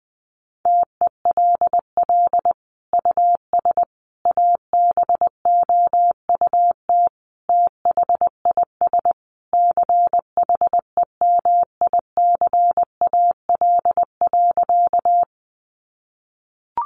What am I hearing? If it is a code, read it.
Morse code, heard as a keyed tone, about 19 wpm, 700 Hz. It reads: TELL US ABOUT THIS CHEMICAL.
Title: Sentences from Top 1200 Words 20wpm